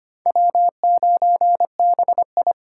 Morse code, heard as a keyed tone, about 25 wpm, 700 Hz. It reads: W9BI
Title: USA Call Signs - 4 Characters - 3x Character Spacing - 25wpm